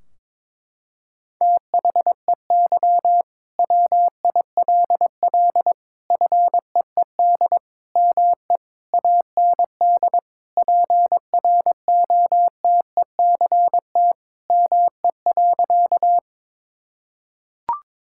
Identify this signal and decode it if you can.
Morse code, heard as a keyed tone, about 22 wpm, 700 Hz. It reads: THEY WILL FEED ME AND PROTECT ME.
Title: Sentences from Top 1600 Words 22wpm